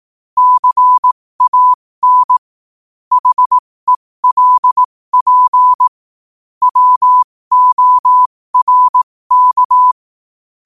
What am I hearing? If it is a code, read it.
Morse code, heard as a keyed tone, about 18 wpm, spaced as for 15 wpm, 1000 Hz. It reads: CAN HELP WORK
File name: 15wpm